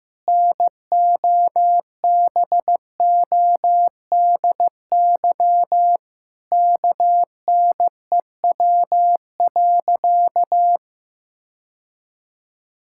Morse code, heard as a keyed tone, about 15 wpm, 700 Hz. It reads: NOBODY KNEW.
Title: Sentences from Top 1600 Words 15wpm